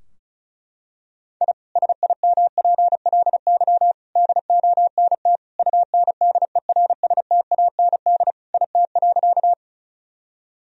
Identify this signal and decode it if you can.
Morse code, heard as a keyed tone, about 36 wpm, 700 Hz. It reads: I SIMPLY DONT UNDERSTAND IT.
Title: Sentences from Top 1000 Words - Review Part 4 - 35wpm